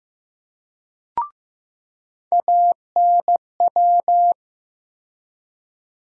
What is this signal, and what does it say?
Morse code, heard as a keyed tone, about 15 wpm, 700 Hz. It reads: ANW
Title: USA Call Sign Postfixes - 3 Letters 15wpm